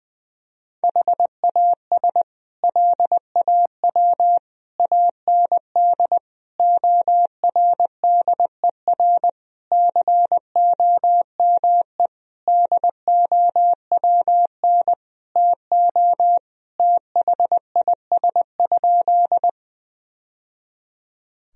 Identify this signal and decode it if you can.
Morse code, heard as a keyed tone, about 21 wpm, 700 Hz. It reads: HAS LAW AND ORDER COME DOWN TO THIS?
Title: Sentences from Top 1000 Words - Review Part 3 - 20wpm